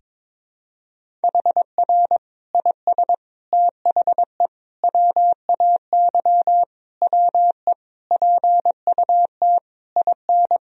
Morse code, heard as a keyed tone, about 22 wpm, 700 Hz. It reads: HR IS THE WAY WE PUT IN
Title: Sentences from Top 200 Words and Abbreviations 22wpm